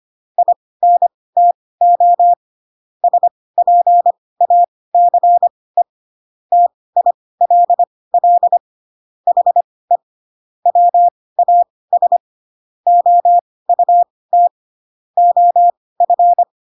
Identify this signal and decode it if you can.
Morse code, heard as a keyed tone, about 25 wpm, spaced as for 18 wpm, 700 Hz. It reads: INTO SPACE TILL HE WAS OUT OF